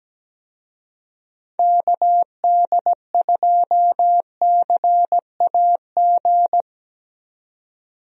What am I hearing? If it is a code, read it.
Morse code, heard as a keyed tone, about 17 wpm, 700 Hz. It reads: KD2CAG
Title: USA Call Signs - 6 Characters - 17wpm